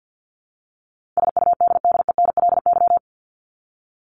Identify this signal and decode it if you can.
Morse code, heard as a keyed tone, about 50 wpm, 700 Hz. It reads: SUDDENLY
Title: LY Suffix - Top 100 Words with Base Words 50wpm